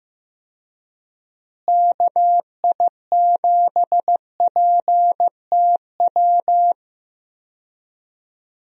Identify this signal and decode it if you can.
Morse code, heard as a keyed tone, about 15 wpm, 700 Hz. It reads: KI7PTW
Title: USA Call Signs - 6 Characters - Repeated 2x - 15wpm